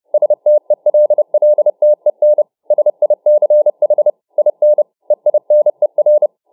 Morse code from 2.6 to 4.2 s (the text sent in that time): SICH